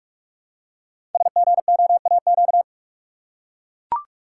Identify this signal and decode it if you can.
Morse code, heard as a keyed tone, about 44 wpm, 700 Hz. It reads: SGOAQ
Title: Code Groups - 5 Letters - 45wpm